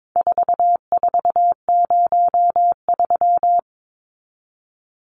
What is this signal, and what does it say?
Morse code, heard as a keyed tone, about 22 wpm, 700 Hz. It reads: 4403
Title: Numbers - 4-Digit 22wpm